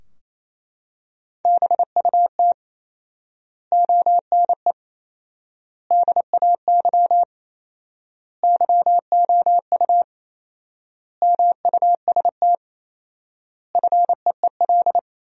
Morse code, heard as a keyed tone, about 28 wpm, 700 Hz. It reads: BUT ONE DAY YOU MUST FEEL
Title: 4x Word-Spacing - Sentences from Top 400 Words - 28wpm